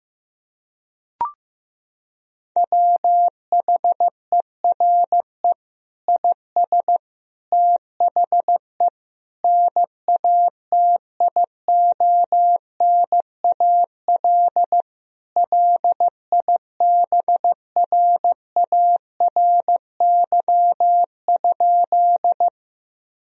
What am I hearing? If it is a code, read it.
Morse code, heard as a keyed tone, about 15 wpm, 700 Hz. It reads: WHERE IS THE NATIONAL LIBRARY?
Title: Sentences from Top 1000 Words - Review Part 4 - 15wpm